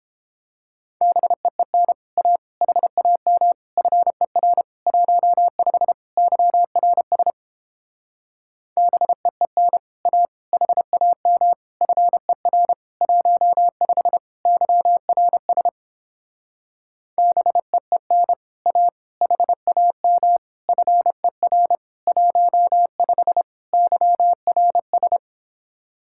Morse code, heard as a keyed tone, about 29 wpm, 700 Hz. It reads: BEEN A HAM FER 15 YRS BEEN A HAM FER 15 YRS BEEN A HAM FER 15 YRS